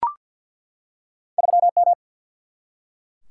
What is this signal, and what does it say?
Morse code, heard as a keyed tone, about 47 wpm, 700 Hz. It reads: 3 M
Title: Single-Letter Single-Number 50wpm